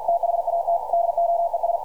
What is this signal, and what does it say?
Morse code, heard as a keyed tone, about 52 wpm, 680 Hz, with radio noise. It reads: THAT DOES
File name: archived / CRSnet